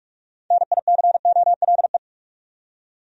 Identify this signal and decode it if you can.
Morse code, heard as a keyed tone, about 46 wpm, 700 Hz. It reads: NICOLE
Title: USA Common Names - 45wpm